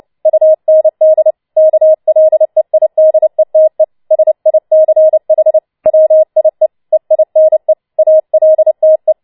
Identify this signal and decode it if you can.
Morse code, heard as a keyed tone, about 29 wpm, 630 Hz. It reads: UND KLEIDETE SICH WIE EINE ALTE
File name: Schneewittchen Teil3 30wpm